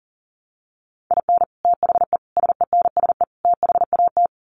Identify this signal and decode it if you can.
Morse code, heard as a keyed tone, about 40 wpm, 700 Hz. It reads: IN THE SENSE THAT
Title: Sets of 4 Words 40wpm